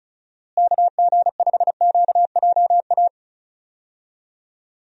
Morse code, heard as a keyed tone, about 36 wpm, 700 Hz. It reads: KG5QJA
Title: USA Call Signs - 6 Characters - Repeated 3x - 35wpm